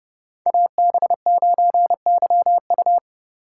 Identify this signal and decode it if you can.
Morse code, heard as a keyed tone, about 33 wpm, 700 Hz. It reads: AB9YU